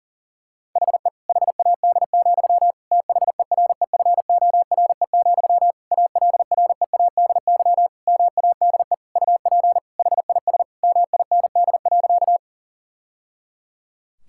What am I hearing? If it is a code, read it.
Morse code, heard as a keyed tone, about 40 wpm, 700 Hz. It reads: HE HAD, THEREFORE, ALREADY MADE UP HIS MIND.
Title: Sentences from Top 1000 Words - Review Part 5 - 40wpm